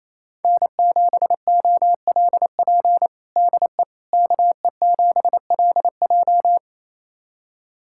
Morse code, heard as a keyed tone, about 28 wpm, 700 Hz. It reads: N7OLP DE KE7LJ
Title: USA Callsign DE Callsign - 5 Characters - 28wpm